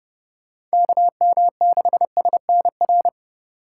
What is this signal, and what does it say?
Morse code, heard as a keyed tone, about 30 wpm, 700 Hz. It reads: KM6SNR